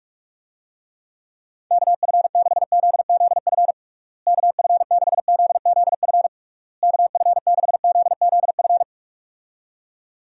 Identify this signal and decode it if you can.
Morse code, heard as a keyed tone, about 46 wpm, 700 Hz. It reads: KF6ZZF KF6ZZF KF6ZZF
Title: USA Call Signs - 6 Characters - Repeated 3x - 45wpm